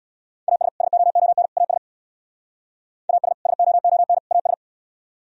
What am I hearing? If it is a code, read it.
Morse code, heard as a keyed tone, about 63 wpm, 700 Hz. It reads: AS SOON AS AS SOON AS